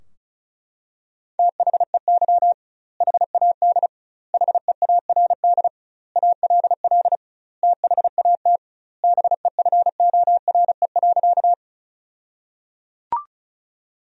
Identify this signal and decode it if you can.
Morse code, heard as a keyed tone, about 35 wpm, 700 Hz. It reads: THEY HAD HEARD ALL THAT BEFORE.
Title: 2x Word-Spacing - Sentences from Top 400 Words - 35wpm